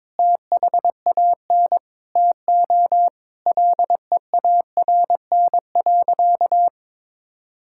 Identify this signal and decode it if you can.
Morse code, heard as a keyed tone, about 22 wpm, 700 Hz. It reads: THAN TO LEARN.
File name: Sentences with 7-Word Phrases - 22wpm